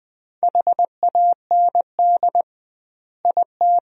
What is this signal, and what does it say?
Morse code, heard as a keyed tone, about 20 wpm, 700 Hz. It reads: HAND IT
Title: 2x Word-Spacing - Sentences from Top 200 Words - 20wpm